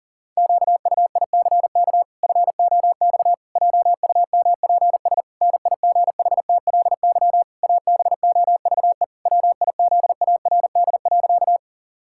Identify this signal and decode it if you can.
Morse code, heard as a keyed tone, about 40 wpm, 700 Hz. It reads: QUICK FOX JUMPS NIGHTLY ABOVE WIZARD.
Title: Pangram Challenge v2 - 40wpm